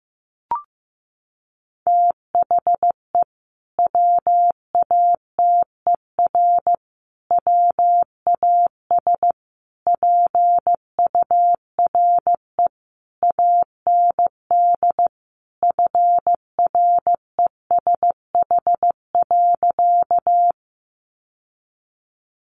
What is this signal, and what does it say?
Morse code, heard as a keyed tone, about 15 wpm, 700 Hz. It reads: THE WATER WAS PURE AND FRESH.
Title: Sentences from Top 1500 Words 15wpm